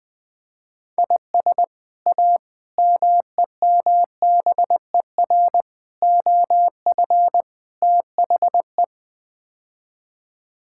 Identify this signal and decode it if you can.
Morse code, heard as a keyed tone, about 20 wpm, 700 Hz. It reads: IS A MEMBER OF THE
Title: Build A Phrase - 5-Word Phrases - 20wpm